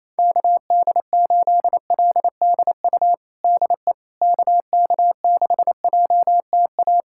Morse code, heard as a keyed tone, about 28 wpm, 700 Hz. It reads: KD8LDU DE KK6JTA
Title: USA Callsign DE Callsign - 6 Characters - 28wpm